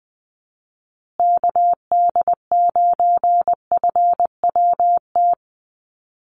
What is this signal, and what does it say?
Morse code, heard as a keyed tone, about 20 wpm, 700 Hz. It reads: KD9FWT
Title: USA Call Signs Repeated 2x 20wpm